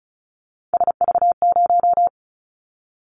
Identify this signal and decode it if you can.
Morse code, heard as a keyed tone, about 36 wpm, 690 Hz. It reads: SV0
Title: Callsign Prefixes for Top 50 DXCC - 35wpm